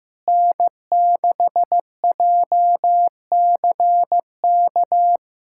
Morse code, heard as a keyed tone, about 15 wpm, 700 Hz. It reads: N6JCK
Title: USA Callsign DE Callsign - 5 Characters - 15wpm